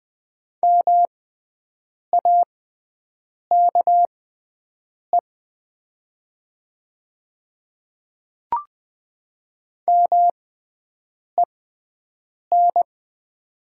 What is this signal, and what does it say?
Morse code, heard as a keyed tone, about 20 wpm, spaced as for 7 wpm, 700 Hz. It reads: MAKE MEN
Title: 6x Character Spacing - Top 200 Words - 20wpm